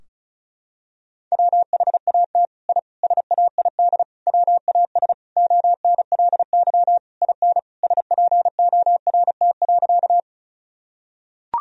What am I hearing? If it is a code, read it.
Morse code, heard as a keyed tone, about 36 wpm, 700 Hz. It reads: WHAT I SAID WAS ONLY IN SPORT.
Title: Sentences from Top 1000 Words - Review Part 4 - 35wpm